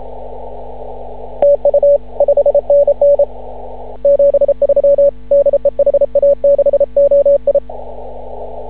Morse code, heard as a keyed tone, about 33 wpm, 560 Hz. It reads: TU 5NN 73 DE HA6OI
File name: CW / ha6oi 2